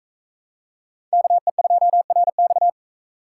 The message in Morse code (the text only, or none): KE2RX